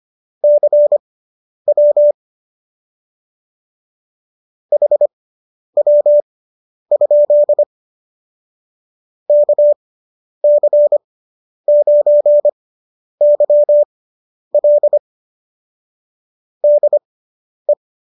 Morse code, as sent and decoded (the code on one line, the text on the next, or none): -.-. .-- / .... .-- ..--.. / -.- -.-. ----. -.-- .-.. / -.. .
CW HW? KC9YL DE